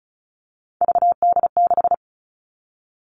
Morse code, heard as a keyed tone, about 36 wpm, 700 Hz. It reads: VD6
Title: Three Challenging Letter-Number Combinations 35wpm